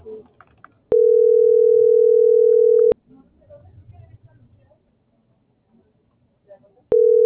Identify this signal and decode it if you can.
Ringback tone (North America), 440 + 480 Hz.